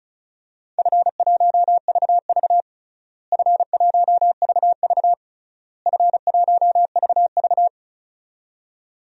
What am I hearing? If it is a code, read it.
Morse code, heard as a keyed tone, about 34 wpm, 700 Hz. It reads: F1VV F1VV F1VV
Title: DXCC-100 Call Signs - 4 Characters - Repeated 3x - 35wpm